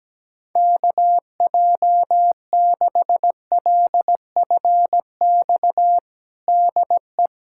Morse code, heard as a keyed tone, about 17 wpm, 700 Hz. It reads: KJ6LFX DE
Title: USA Callsign DE Callsign - 6 Characters - 17wpm